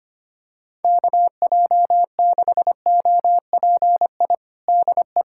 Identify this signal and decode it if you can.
Morse code, heard as a keyed tone, about 25 wpm, 700 Hz. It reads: KJ6OPI DE
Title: USA Callsign DE Callsign - 6 Characters - 25wpm